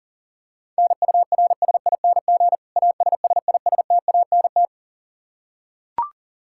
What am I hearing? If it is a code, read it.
Morse code, heard as a keyed tone, about 37 wpm, 700 Hz. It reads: NURSING ASSISTANT
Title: Practice QSO - Exchange 3 - More Info - 40wpm